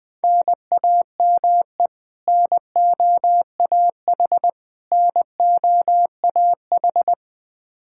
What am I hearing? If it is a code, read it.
Morse code, heard as a keyed tone, about 20 wpm, 700 Hz. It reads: NAME NOAH NOAH
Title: Practice QSO - Exchange 1 - Beginning - 20wpm